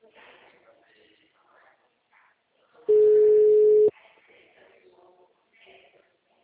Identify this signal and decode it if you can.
Ringback tone (Europe, 425 Hz).
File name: out- -2002- - - .1304